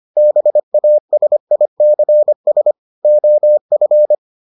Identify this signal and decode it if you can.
Morse code, heard as a keyed tone, about 25 wpm, 600 Hz. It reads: BASICS OF